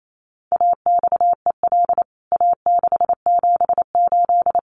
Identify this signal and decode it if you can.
Morse code, heard as a keyed tone, about 28 wpm, 700 Hz. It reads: AXEL A678
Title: CWT Exchange 28wpm